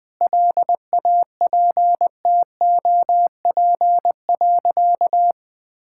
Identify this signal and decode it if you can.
Morse code, heard as a keyed tone, about 20 wpm, 700 Hz. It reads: LAPTOP.
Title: Easy Sentences with Hints - 20wpm